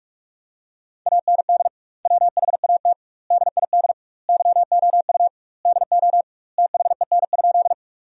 Morse code, heard as a keyed tone, about 46 wpm, 700 Hz. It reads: AND WHAT DID YOU DO THEN?
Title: 2x Word-Spacing - Sentences from Top 100 Words - 45wpm